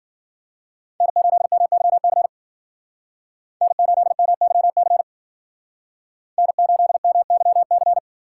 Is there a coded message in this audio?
Yes: Morse code from 1.0 to 8.0 s, heard as a keyed tone, about 55 wpm, 700 Hz. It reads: N8MYC N8MYC N8MYC